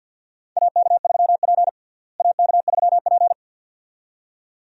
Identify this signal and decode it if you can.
Morse code, heard as a keyed tone, about 47 wpm, 700 Hz. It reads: AK3P AK3P
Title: USA Call Signs - 4 Characters - Repeated 2x - 50wpm